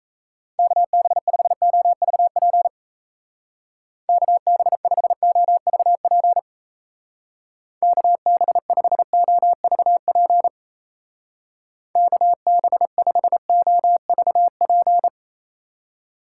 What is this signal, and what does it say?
Morse code, heard as a keyed tone, about 35 wpm, 700 Hz. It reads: KB5OVP KB5OVP KB5OVP KB5OVP
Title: Speed Racing - USA 2x3 Call Signs 28wpm